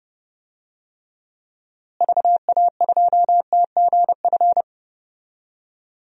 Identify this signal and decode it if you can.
Morse code, heard as a keyed tone, about 32 wpm, 700 Hz. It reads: VA2TGF
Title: DXCC-100 Call Signs - 6 Characters - Repeated 3x - 30wpm